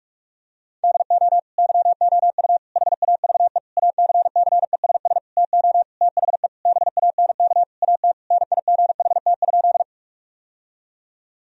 Morse code, heard as a keyed tone, about 46 wpm, 700 Hz. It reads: DO YOU HAVE ACCESS TO THE BANK AT NIGHT?
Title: Sentences from Top 1000 Words - Review Part 4 - 45wpm